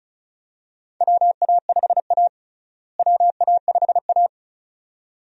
Morse code, heard as a keyed tone, about 36 wpm, 700 Hz. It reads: WA5A WA5A
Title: USA Call Signs - 4 Characters - Repeated 2x - 35wpm